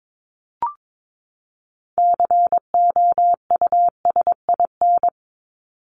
Morse code, heard as a keyed tone, about 22 wpm, 700 Hz. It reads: COUSIN
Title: Top 1501-1750 Words 22wpm